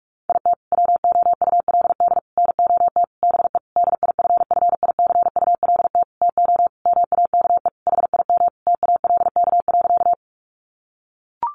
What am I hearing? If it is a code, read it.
Morse code, heard as a keyed tone, about 44 wpm, 700 Hz. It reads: IT WOULD NOT BE DIFFICULT TO MAKE HIM TALK.
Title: Sentences from Top 600 Words 45wpm